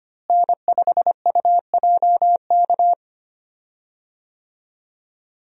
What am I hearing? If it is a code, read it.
Morse code, heard as a keyed tone, about 25 wpm, 700 Hz. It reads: N5UJK
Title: USA Call Signs - 5 Characters - Repeated 3x - 25wpm